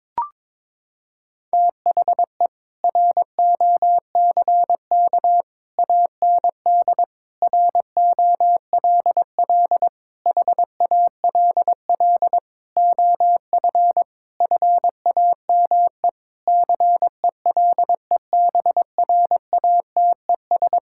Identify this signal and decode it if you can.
Morse code, heard as a keyed tone, about 21 wpm, 700 Hz. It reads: THE ROCK AND ROLL HALL OF FAME CELEBRATES
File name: Sentences with 7-Word Phrases - 22wpm